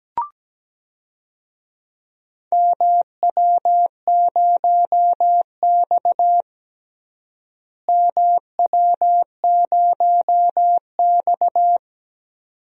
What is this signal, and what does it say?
Morse code, heard as a keyed tone, about 17 wpm, 700 Hz. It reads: MW0X MW0X
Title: DXCC-100 Call Signs - 4 Characters - Repeated 2x - 17wpm